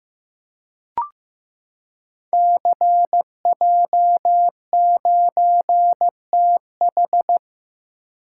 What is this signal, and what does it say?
Morse code, heard as a keyed tone, about 15 wpm, 700 Hz. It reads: CJ9TH
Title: Code Groups - 5 Letter:Numbers - 15wpm